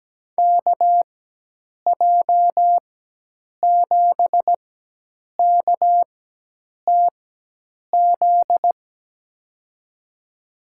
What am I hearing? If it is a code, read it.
Morse code, heard as a keyed tone, about 17 wpm, spaced as for 8 wpm, 700 Hz. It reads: KJ7KTZ
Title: USA Call Signs - 6 Characters - 4x Character Spacing - 17wpm